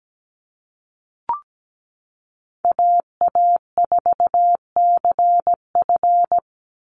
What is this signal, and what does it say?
Morse code, heard as a keyed tone, about 17 wpm, 700 Hz. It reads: AA4CF
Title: Call Signs Repeated 2x - 17wpm